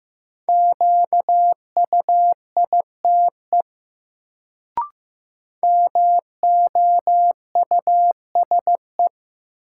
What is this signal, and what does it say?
Morse code, heard as a keyed tone, about 15 wpm, 700 Hz. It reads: QUITE MOUSE
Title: Sentences from Top 700 Words - New Word Drill - 15wpm